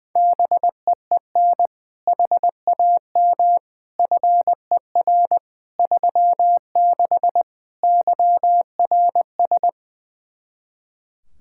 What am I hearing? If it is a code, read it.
Morse code, heard as a keyed tone, about 20 wpm, 700 Hz. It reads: BEEN HAM FER 36 YRS
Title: Practice QSO - Exchange 3 - More Info - 20wpm